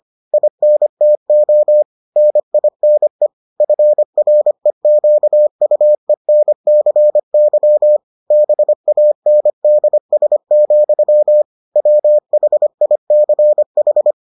Morse code, heard as a keyed tone, about 25 wpm, 600 Hz. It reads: INTO NINE FREQUENCY BANDS, WHICH